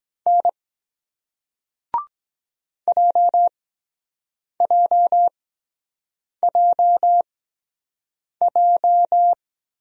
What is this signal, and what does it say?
Morse code, heard as a keyed tone, about 23 wpm, spaced as for 7 wpm, 700 Hz. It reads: N JJJJ